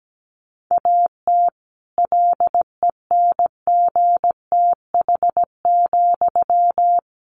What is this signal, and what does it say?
Morse code, heard as a keyed tone, about 17 wpm, 700 Hz. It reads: AT LENGTH,
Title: Sentences from Top 1000 Words 17wpm